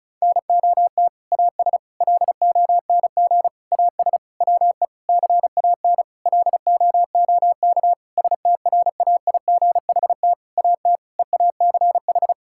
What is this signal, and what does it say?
Morse code, heard as a keyed tone, about 34 wpm, 700 Hz. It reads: NOT AS LONG AS WE CAN LOOK STRAIGHT AT EACH
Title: Sentences from Top 1000 Words - Review Part 6 - 35wpm